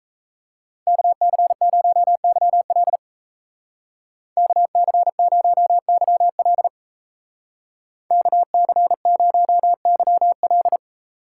Morse code, heard as a keyed tone, about 33 wpm, 700 Hz. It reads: KC0YL KC0YL KC0YL
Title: Speed Racing - USA 2x2 Call Signs 28wpm